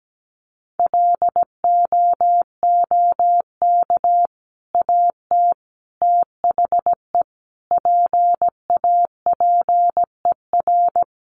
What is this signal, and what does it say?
Morse code, heard as a keyed tone, about 17 wpm, 700 Hz. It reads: LOOK AT THE PAPER
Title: Sentences from Top 400 Words 17wpm